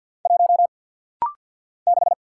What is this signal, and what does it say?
Morse code, heard as a keyed tone, about 48 wpm, 690 Hz. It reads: 1 6
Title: Single Letter-Number Alt - Rapid-Fire 50wpm